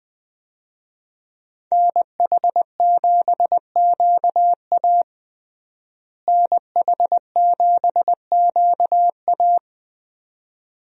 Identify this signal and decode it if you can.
Morse code, heard as a keyed tone, about 20 wpm, 700 Hz. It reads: NH7QA NH7QA
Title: USA Call Signs - 5 Characters - Repeated 2x - 20wpm